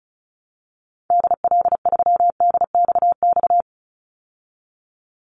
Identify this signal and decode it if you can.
Morse code, heard as a keyed tone, about 34 wpm, 700 Hz. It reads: DL3DXX
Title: Call Signs Repeated 3x - 35wpm